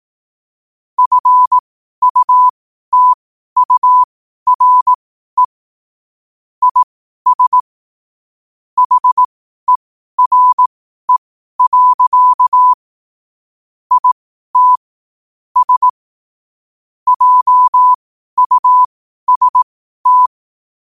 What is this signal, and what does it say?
Morse code, heard as a keyed tone, about 18 wpm, spaced as for 12 wpm, 1000 Hz. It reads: FUTURE IS HERE. IT S JUST
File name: qotdfiles / 12wpm